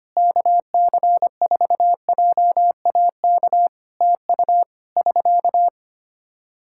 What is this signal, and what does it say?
Morse code, heard as a keyed tone, about 25 wpm, 700 Hz. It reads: KC4JAK TU <SK>
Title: Practice QSO - Exchange 4 - Ending - 25wpm